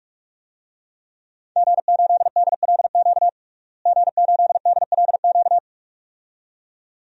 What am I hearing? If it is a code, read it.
Morse code, heard as a keyed tone, about 46 wpm, 700 Hz. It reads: G8DLQ G8DLQ
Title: DXCC-100 Call Signs - 5 Characters - Repeated 2x - 45wpm